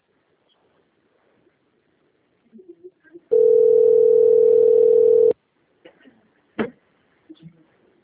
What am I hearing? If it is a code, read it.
Ringback tone (North America), 440 + 480 Hz.